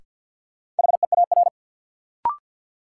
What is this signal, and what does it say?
Morse code, heard as a keyed tone, about 53 wpm, 700 Hz. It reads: HEAR
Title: Sets of 5 Words - New Word Drill - 50wpm